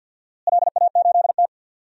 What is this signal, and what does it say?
Morse code, heard as a keyed tone, about 53 wpm, 700 Hz. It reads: LA8T
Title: DXCC-100 Call Signs - 4 Characters - Repeated 3x - 50wpm